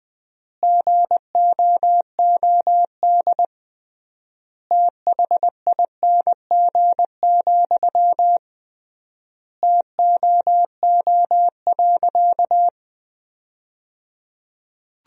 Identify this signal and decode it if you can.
Morse code, heard as a keyed tone, about 20 wpm, 700 Hz. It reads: GOOD THING, TOO.
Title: 3x Word-Spacing - Sentences from Top 200 Words - 20wpm